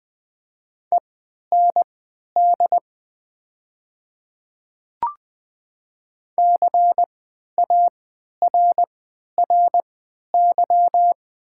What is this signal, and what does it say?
Morse code, heard as a keyed tone, about 20 wpm, spaced as for 11 wpm, 700 Hz. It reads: END CARRY
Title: 3x Character Spacing - Top 300 Words - 20wpm